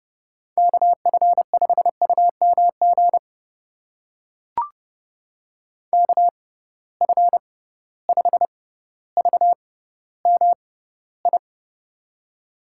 Morse code, heard as a keyed tone, about 32 wpm, spaced as for 16 wpm, 700 Hz. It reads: KF5UMG KF5VMI
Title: USA Call Signs - 6 Characters - 6x Character Spacing - 30wpm